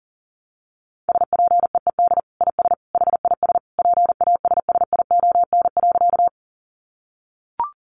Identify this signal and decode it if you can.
Morse code, heard as a keyed tone, about 40 wpm, 700 Hz. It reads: SPEED IS HIS PASSION.
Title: Sentences from Top 1300 Words 40wpm